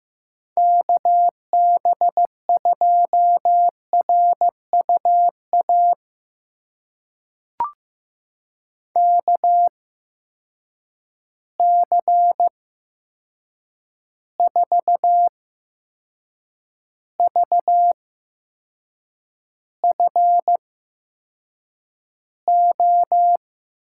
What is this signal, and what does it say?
Morse code, heard as a keyed tone, about 15 wpm, spaced as for 6 wpm, 700 Hz. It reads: KB2RUA KC4VFO
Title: USA Call Signs - 8x Character Spacing - 15wpm